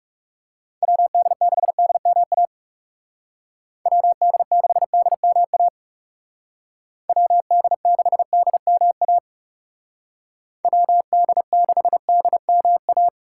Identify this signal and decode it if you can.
Morse code, heard as a keyed tone, about 36 wpm, 700 Hz. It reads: WD6DMA WD6DMA WD6DMA WD6DMA
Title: Speed Racing - USA 2x3 Call Signs 30wpm